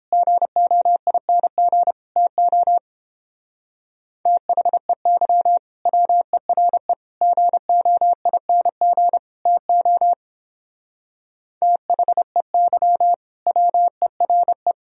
Morse code, heard as a keyed tone, about 29 wpm, 700 Hz. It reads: GOING TO THEY WERE GOING TO THEY WERE